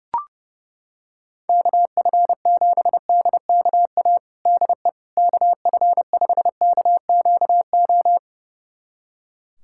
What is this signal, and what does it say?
Morse code, heard as a keyed tone, about 32 wpm, 700 Hz. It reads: KF7DKA DE KF5KQO
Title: USA Callsign DE Callsign - 6 Characters - 30wpm